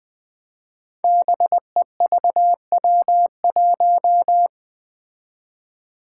Morse code, heard as a keyed tone, about 20 wpm, 700 Hz. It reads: BEVW1
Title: Five Challenging Letter-Number Combinations 20wpm